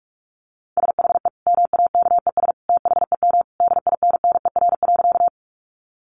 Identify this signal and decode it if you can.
Morse code, heard as a keyed tone, about 44 wpm, 700 Hz. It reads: SHE MAKES THEM DINNER.
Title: Easy Phrases 45wpm